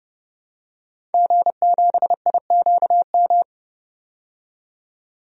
Morse code, heard as a keyed tone, about 30 wpm, 700 Hz. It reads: G7IQM
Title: DXCC-100 Call Signs - 5 Characters - 30wpm